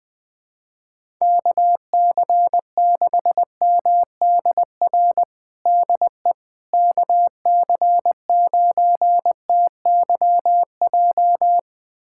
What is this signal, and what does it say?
Morse code, heard as a keyed tone, about 21 wpm, 700 Hz. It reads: KC6MDR DE KC9TYJ
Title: USA Callsign DE Callsign - 6 Characters - 20wpm